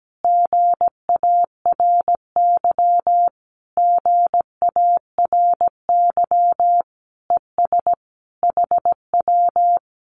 Morse code, heard as a keyed tone, about 17 wpm, 700 Hz. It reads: GARY GARY ES HW
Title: Common QSO Phrases Encore 17wpm